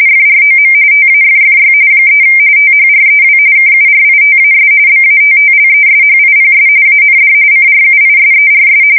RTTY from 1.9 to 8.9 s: 3 THREW KURITAS BATTLESHIPS AND CRUISERS